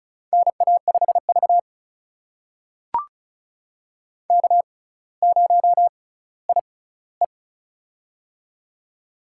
Morse code, heard as a keyed tone, about 36 wpm, spaced as for 18 wpm, 700 Hz. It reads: NA5V K0IE
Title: USA Call Signs - 6x Character Spacing - 35wpm